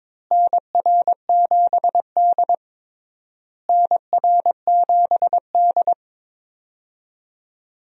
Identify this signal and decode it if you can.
Morse code, heard as a keyed tone, about 21 wpm, 700 Hz. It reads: NR7D NR7D
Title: USA Call Signs - 4 Characters - Repeated 3x - 22wpm